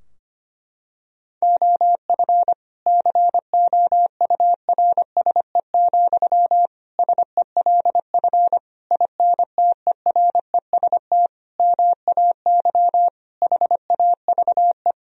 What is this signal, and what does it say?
Morse code, heard as a keyed tone, about 25 wpm, 700 Hz. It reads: OF COURSE, SELF INTEREST MAY HAVE
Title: Sentences from Top 1000 Words - Review Part 2 - 25wpm